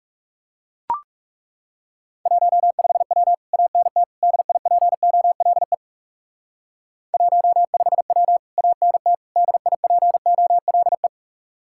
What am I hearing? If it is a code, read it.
Morse code, heard as a keyed tone, about 36 wpm, 700 Hz. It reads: 15W ANT DIPOLE 15W ANT DIPOLE
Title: Speed-Racing Common QSO Phrases Encore - 30wpm